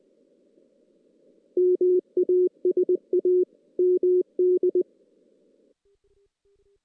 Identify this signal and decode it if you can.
Morse code, heard as a keyed tone, about 23 wpm, 370 Hz. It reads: MASA MD
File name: 2016 JAN CW NAQP / 2016 Jan naqp k5ot - 0-5